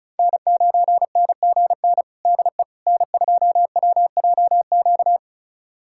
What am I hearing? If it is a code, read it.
Morse code, heard as a keyed tone, about 36 wpm, 700 Hz. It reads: N9NGN DE N2WJQ
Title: USA Callsign DE Callsign 35wpm